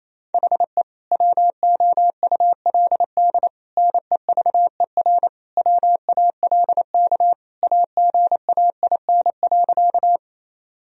Morse code, heard as a keyed tone, about 28 wpm, 700 Hz. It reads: HE WOULD NEVER WALK AGAIN.
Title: Sentences from Top 1000 Words - Review Part 1 - 28wpm